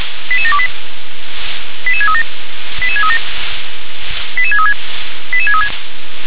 Five-tone selective call (ZVEI), heard as quick runs of five tones, tones of about 70 ms. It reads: 88728 88637 88637 88636 88636